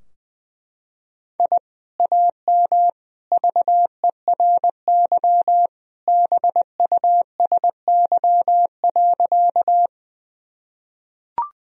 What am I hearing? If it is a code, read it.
Morse code, heard as a keyed tone, about 21 wpm, 700 Hz. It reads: I AM VERY BUSY.
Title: Sentences from Top 1000 Words - Review Part 3 - 20wpm